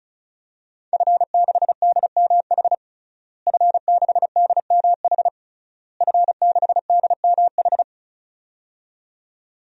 Morse code, heard as a keyed tone, about 36 wpm, 700 Hz. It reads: F6DMH F6DMH F6DMH
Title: DXCC-100 Call Signs - 5 Characters - Repeated 3x - 35wpm